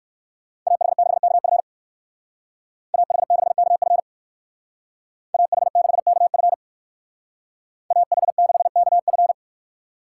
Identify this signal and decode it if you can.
Morse code, heard as a keyed tone, about 56 wpm, 690 Hz. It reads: AH6KF AH6KF AH6KF AH6KF